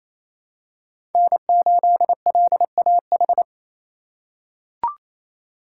Morse code, heard as a keyed tone, about 28 wpm, 700 Hz. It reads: N8LAH
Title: USA Call Signs - 5 Characters - 4x Character Spacing - 28wpm